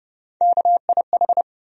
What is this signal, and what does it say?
Morse code, heard as a keyed tone, about 31 wpm, 700 Hz. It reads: KIH
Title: USA Call Sign Postfixes - 3 Letters 30wpm